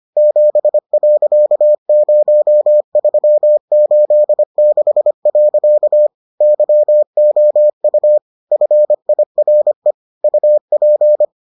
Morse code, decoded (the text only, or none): 7.0386. YOU FIRE UP